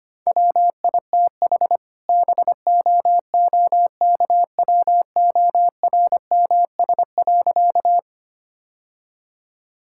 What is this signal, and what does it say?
Morse code, heard as a keyed tone, about 25 wpm, 700 Hz. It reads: WITH BOOKWORMS.
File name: practice / 25wpm / All-Age Appropriate Jokes - 25wpm